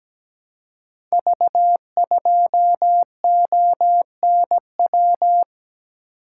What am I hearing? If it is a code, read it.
Morse code, heard as a keyed tone, about 17 wpm, 700 Hz. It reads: V2ONW
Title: Code Groups - 5 Letter:Numbers - 17wpm